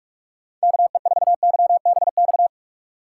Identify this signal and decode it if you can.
Morse code, heard as a keyed tone, about 46 wpm, 700 Hz. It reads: KE4YBX
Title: USA Call Signs - 6 Characters - 4x Character Spacing - 45wpm